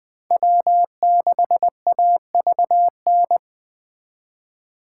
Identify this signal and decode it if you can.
Morse code, heard as a keyed tone, about 20 wpm, 700 Hz. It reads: W6AVN